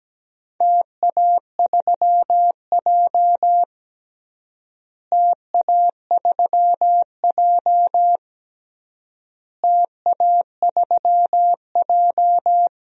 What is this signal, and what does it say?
Morse code, heard as a keyed tone, about 17 wpm, 700 Hz. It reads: TA3J TA3J TA3J
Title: DXCC-100 Call Signs - 4 Characters - Repeated 3x - 17wpm